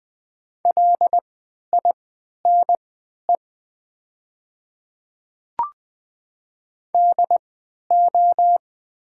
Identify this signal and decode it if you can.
Morse code, heard as a keyed tone, about 20 wpm, spaced as for 11 wpm, 700 Hz. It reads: LINE DO
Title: 3x Character Spacing - Top 300 Words - 20wpm